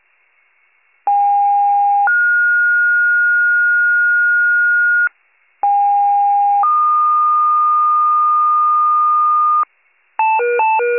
Two-tone page, heard as a one-second tone, then a three-second tone, about 810 then 1430 Hz, about 810 then 1220 Hz, with noise. It is heard from 1.1 to 9.6 s.